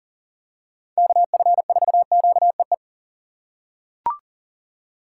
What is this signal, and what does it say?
Morse code, heard as a keyed tone, about 38 wpm, 700 Hz. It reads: KF4QEE
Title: USA Call Signs - 6x Character Spacing - 40wpm